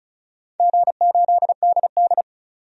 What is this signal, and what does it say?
Morse code, heard as a keyed tone, about 36 wpm, 700 Hz. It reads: G8DD
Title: DXCC-100 Call Signs - 4 Characters - 35wpm